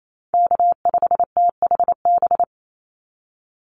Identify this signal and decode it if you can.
Morse code, heard as a keyed tone, about 28 wpm, 700 Hz. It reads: K5THB
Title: USA Call Signs 28wpm